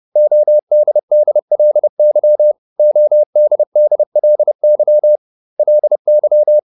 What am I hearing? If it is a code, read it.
Morse code, heard as a keyed tone, about 30 wpm, 600 Hz. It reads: ODDLY ODDLY LY